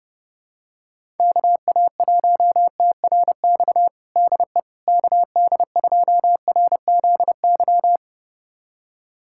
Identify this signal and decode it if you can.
Morse code, heard as a keyed tone, about 30 wpm, 700 Hz. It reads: KA1TRX DE KD2RZY
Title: USA Callsign DE Callsign - 6 Characters - 30wpm